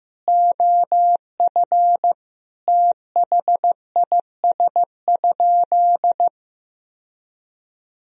Morse code, heard as a keyed tone, about 14 wpm, 700 Hz. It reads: OF THIS?
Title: Sentences from Top 1000 Words - Review Part 1 - 15wpm